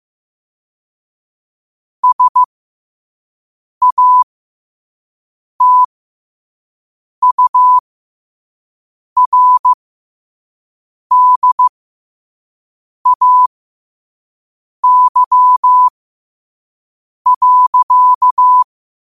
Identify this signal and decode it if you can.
Morse code, heard as a keyed tone, about 15 wpm, spaced as for 5 wpm, 1000 Hz. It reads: SATURDAY.
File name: qotdfiles / 5wpm